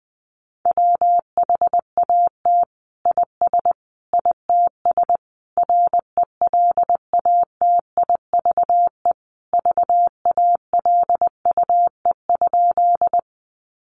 Morse code, heard as a keyed tone, about 20 wpm, 700 Hz. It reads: WHAT IS ITS RELATIVE VALUE?
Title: Sentences from Top 1400 Words 20wpm